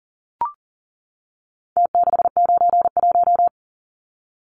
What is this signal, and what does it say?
Morse code, heard as a keyed tone, about 40 wpm, 700 Hz. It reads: T691
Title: Numbers - Contest Style - 4-Digit 40wpm